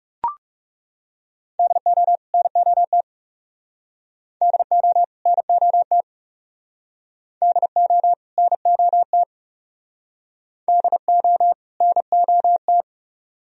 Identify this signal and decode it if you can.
Morse code, heard as a keyed tone, about 38 wpm, 700 Hz. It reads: DO NOT DO NOT DO NOT DO NOT
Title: Speed-Racing - Sets of 2 Words 30wpm